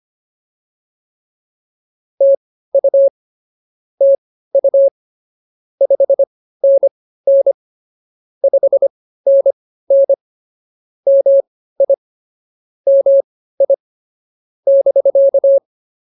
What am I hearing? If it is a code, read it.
Morse code, heard as a keyed tone, about 24 wpm, spaced as for 15 wpm, 550 Hz. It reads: TU TU 5NN 5NN MI MI <BK>